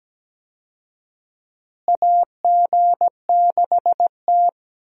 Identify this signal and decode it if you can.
Morse code, heard as a keyed tone, about 17 wpm, 700 Hz. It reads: AG6T
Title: USA Call Signs - 4 Characters - Repeated 2x - 17wpm